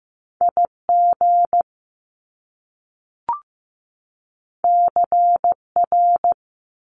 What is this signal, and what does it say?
Morse code, heard as a keyed tone, about 15 wpm, 700 Hz. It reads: IG CR
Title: Top Two Letter Combinations in Words 15wpm